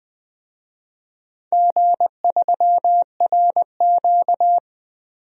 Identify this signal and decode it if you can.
Morse code, heard as a keyed tone, about 20 wpm, 700 Hz. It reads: G3RQ
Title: DXCC-100 Call Signs - 4 Characters - 20wpm